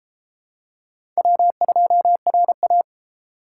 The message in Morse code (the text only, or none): W2RA